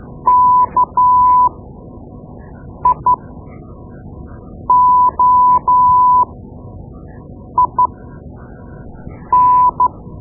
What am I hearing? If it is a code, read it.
Morse code, heard as a keyed tone, about 12 wpm, spaced as for 5 wpm, 1000 Hz. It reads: KIOIN
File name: Morse Lessons / Morse Lessons / Tape-2B